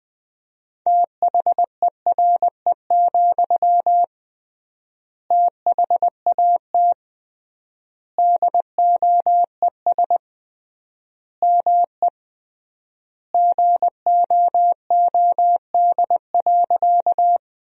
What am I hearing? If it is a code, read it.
Morse code, heard as a keyed tone, about 20 wpm, 700 Hz. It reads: THERE, THAT DOES ME GOOD.
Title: 3x Word-Spacing - Sentences from Top 200 Words - 20wpm